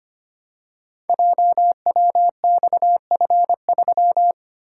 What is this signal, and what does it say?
Morse code, heard as a keyed tone, about 25 wpm, 700 Hz. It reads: JWXF3
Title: Code Groups - 5 Letter:Numbers - 25wpm